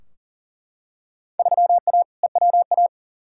Morse code, heard as a keyed tone, about 40 wpm, 700 Hz. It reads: 3A EWA
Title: Field Day Exchanges - 40wpm